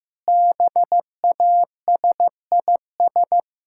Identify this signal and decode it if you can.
Morse code, heard as a keyed tone, about 16 wpm, 700 Hz. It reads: BASIS
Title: Sentences from Top 1600 Words 15wpm